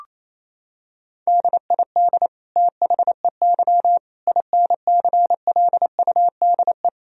Morse code, heard as a keyed tone, about 27 wpm, 700 Hz. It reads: DID THEY INCLUDE
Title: Sentences from Top 1000 Words - Review Part 3 - 28wpm